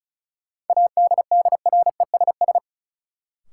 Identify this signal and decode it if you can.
Morse code, heard as a keyed tone, about 34 wpm, 700 Hz. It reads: ADDRESS
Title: Sets of 5 Words Encore - New Word Drill - 35wpm